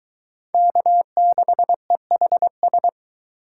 Morse code, heard as a keyed tone, about 23 wpm, 700 Hz. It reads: K6EHS